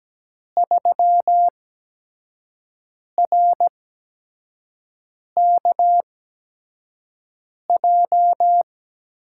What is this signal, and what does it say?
Morse code, heard as a keyed tone, about 17 wpm, spaced as for 5 wpm, 700 Hz. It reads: 3RKJ
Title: USA Call Signs - 8x Character Spacing - 17wpm